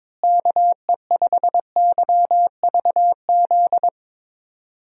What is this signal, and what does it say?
Morse code, heard as a keyed tone, about 22 wpm, 700 Hz. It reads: KE5YVZ
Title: USA Call Signs - 6 Characters - Repeated 3x - 22wpm